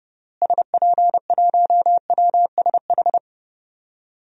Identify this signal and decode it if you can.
Morse code, heard as a keyed tone, about 32 wpm, 700 Hz. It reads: SP1WSH
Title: DXCC-100 Call Signs - 6 Characters - Repeated 3x - 30wpm